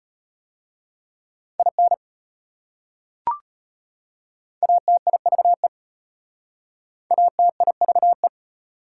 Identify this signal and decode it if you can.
Morse code, heard as a keyed tone, about 34 wpm, 700 Hz. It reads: IN ATIVE ATIVE